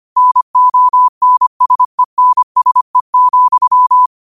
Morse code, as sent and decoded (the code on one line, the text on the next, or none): -. --- -. ... -. ... --..--
NONSENSE,